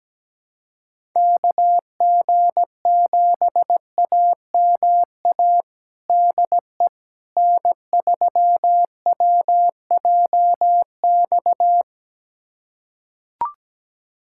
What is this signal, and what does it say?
Morse code, heard as a keyed tone, about 17 wpm, 700 Hz. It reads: KG7AMA DE N3WJX
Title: USA Callsign DE Callsign 17wpm